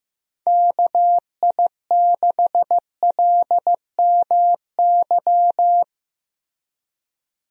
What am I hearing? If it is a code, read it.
Morse code, heard as a keyed tone, about 15 wpm, 700 Hz. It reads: KI6LMY
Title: USA Call Signs - 6 Characters - Repeated 3x - 15wpm